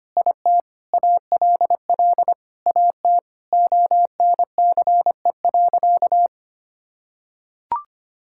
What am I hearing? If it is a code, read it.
Morse code, heard as a keyed tone, about 25 wpm, 700 Hz. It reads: IT ALL AT ONCE.
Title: Sentences from Top 1000 Words - Review Part 5 - 25wpm